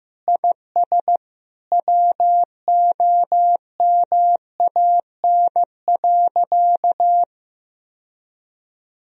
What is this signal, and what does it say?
Morse code, heard as a keyed tone, about 16 wpm, 700 Hz. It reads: IS WOMAN.
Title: Sentences from Top 1000 Words - Review Part 3 - 15wpm